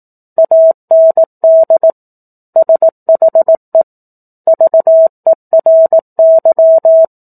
Morse code, heard as a keyed tone, about 18 wpm, 650 Hz. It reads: AND SHE VERY
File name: 04-Chapter-04 18 WPM